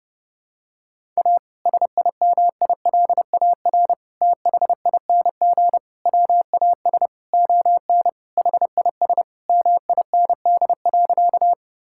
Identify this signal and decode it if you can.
Morse code, heard as a keyed tone, about 32 wpm, 700 Hz. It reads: A SIMILAR THING WAS ON HIS MIND.
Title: Sentences from Top 1000 Words - Review Part 5 - 30wpm